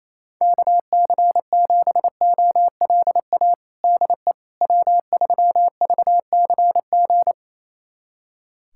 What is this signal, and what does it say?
Morse code, heard as a keyed tone, about 28 wpm, 700 Hz. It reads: KC7OLA DE W3VCG
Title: USA Callsign DE Callsign 28wpm